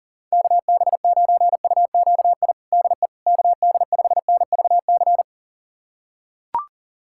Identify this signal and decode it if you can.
Morse code, heard as a keyed tone, about 40 wpm, 700 Hz. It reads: KB9UQI DE KD5NVC
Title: USA Callsign DE Callsign - 6 Characters - 40wpm